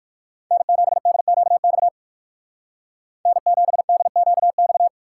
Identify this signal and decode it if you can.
Morse code, heard as a keyed tone, about 60 wpm, 700 Hz. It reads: N7DQX N7DQX